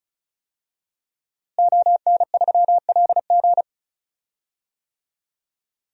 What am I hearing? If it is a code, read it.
Morse code, heard as a keyed tone, about 34 wpm, 700 Hz. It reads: ON3LG
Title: DXCC-100 Call Signs - 5 Characters - Repeated 2x - 35wpm